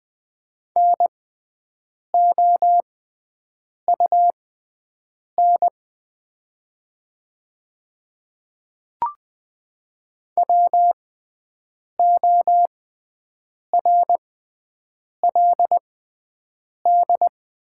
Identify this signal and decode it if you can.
Morse code, heard as a keyed tone, about 20 wpm, spaced as for 7 wpm, 700 Hz. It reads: NOUN WORLD
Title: 6x Character Spacing - Top 500 Words - 20wpm